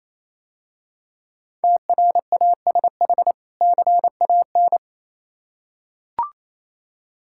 Morse code, heard as a keyed tone, about 27 wpm, 700 Hz. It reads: TRASH CAN
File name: Easy Words - Speed-Racing with Hints - 22wpm